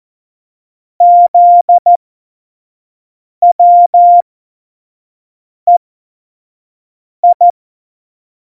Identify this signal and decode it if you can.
Morse code, heard as a keyed tone, about 14 wpm, spaced as for 5 wpm, 700 Hz. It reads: ZWEI
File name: Morsekurs / Witze / Witz 08. Mai 2018, 25 F70